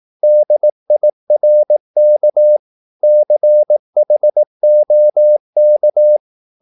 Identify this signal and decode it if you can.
Morse code, heard as a keyed tone, about 18 wpm, 600 Hz. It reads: DIRK CHOK
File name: Korte ord DK 18wpm